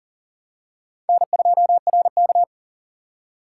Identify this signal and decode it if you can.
Morse code, heard as a keyed tone, about 40 wpm, 700 Hz. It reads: N2RK
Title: USA Call Signs - 4 Characters - 3x Character Spacing - 40wpm